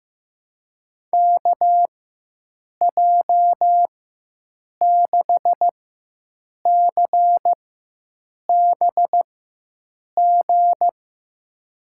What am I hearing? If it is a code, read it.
Morse code, heard as a keyed tone, about 15 wpm, spaced as for 7 wpm, 700 Hz. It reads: KJ6CBG
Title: USA Call Signs - 6 Characters - 4x Character Spacing - 15wpm